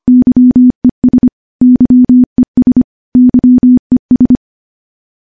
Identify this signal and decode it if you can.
Morse code, heard as a keyed tone, about 25 wpm, 250 Hz. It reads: YES YES YES
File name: morse / morse-training-25WPM